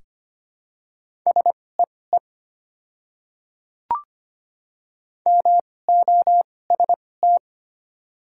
Morse code, heard as a keyed tone, about 25 wpm, spaced as for 18 wpm, 700 Hz. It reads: SEE MOST
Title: 2x Character Spacing - Top 200 Words - 25wpm